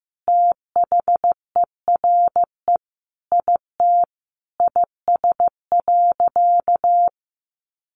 Morse code, heard as a keyed tone, about 15 wpm, 700 Hz. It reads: THERE IT IS.
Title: Sentences from Top 500 Words 15wpm